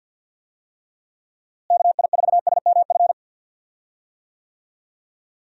Morse code, heard as a keyed tone, about 51 wpm, 700 Hz. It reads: KI4SMF
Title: USA Call Signs - 6 Characters - Repeated 3x - 50wpm